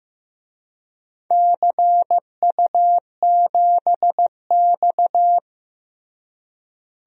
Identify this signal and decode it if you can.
Morse code, heard as a keyed tone, about 15 wpm, 700 Hz. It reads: CU7X
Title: DXCC-100 Call Signs - 4 Characters - Repeated 2x - 15wpm